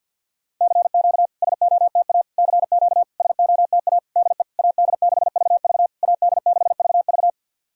Morse code, heard as a keyed tone, about 48 wpm, 690 Hz. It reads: CQ SOTA CQ SOTA DE AD6VV AD6VV
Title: SOTA QSO Practice 50wpm